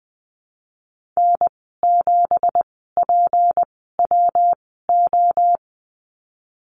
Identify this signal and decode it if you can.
Morse code, heard as a keyed tone, about 20 wpm, spaced as for 14 wpm, 700 Hz. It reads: N7PWO
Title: 173 Lesson 28g - Number 9 - Call Signs 20wpm